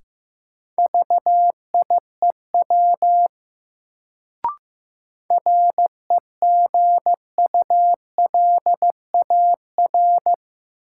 Morse code, heard as a keyed tone, about 16 wpm, 700 Hz. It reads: VIEW REGULAR
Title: Sentences from Top 900 Words - New Word Drill - 15wpm